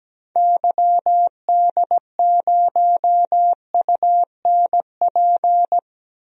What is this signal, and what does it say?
Morse code, heard as a keyed tone, about 17 wpm, 700 Hz. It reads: YD0UNP
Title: DXCC-100 Call Signs - 6 Characters - 17wpm